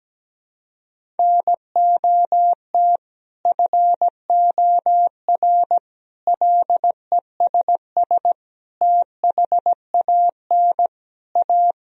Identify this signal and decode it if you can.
Morse code, heard as a keyed tone, about 17 wpm, 700 Hz. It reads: NOT FOR LESS THAN A
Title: Sentences from Top 1000 Words - Review Part 3 - 17wpm